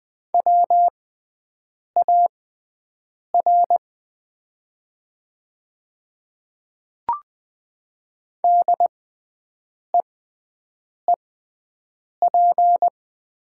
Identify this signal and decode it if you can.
Morse code, heard as a keyed tone, about 20 wpm, spaced as for 7 wpm, 700 Hz. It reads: WAR DEEP
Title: 6x Character Spacing - Top 500 Words - 20wpm